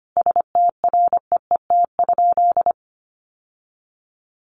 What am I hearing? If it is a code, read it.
Morse code, heard as a keyed tone, about 25 wpm, 700 Hz. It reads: STREET?
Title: Sentences from Top 900 Words 25wpm